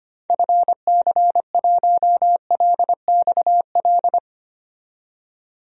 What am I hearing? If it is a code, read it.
Morse code, heard as a keyed tone, about 25 wpm, 700 Hz. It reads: FC1LXL
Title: DXCC-100 Call Signs - 6 Characters - 25wpm